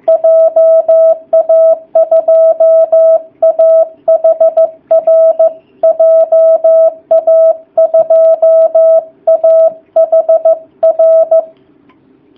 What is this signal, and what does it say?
Morse code, heard as a keyed tone, about 15 wpm, 660 Hz. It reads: JA2AHR JA2AHR
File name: jh8cbh / 2019cw / sp15- w